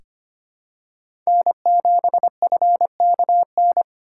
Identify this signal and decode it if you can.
Morse code, heard as a keyed tone, about 25 wpm, 700 Hz. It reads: N7FKN
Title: USA Call Signs - 5 Characters - 6x Character Spacing - 25wpm